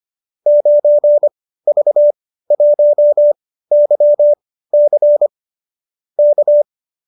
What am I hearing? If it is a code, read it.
Morse code, heard as a keyed tone, about 25 wpm, spaced as for 15 wpm, 590 Hz. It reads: 9V1YC K